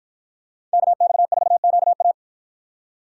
Morse code, heard as a keyed tone, about 53 wpm, 700 Hz. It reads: KX4QA